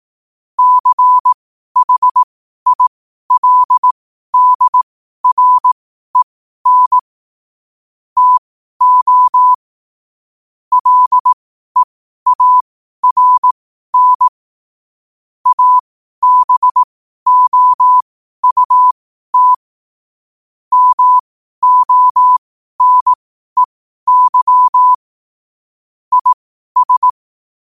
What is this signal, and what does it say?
Morse code, heard as a keyed tone, about 18 wpm, spaced as for 12 wpm, 1000 Hz. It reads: CHILDREN TO LEARN ABOUT MONEY IS
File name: qotdfiles / 12wpm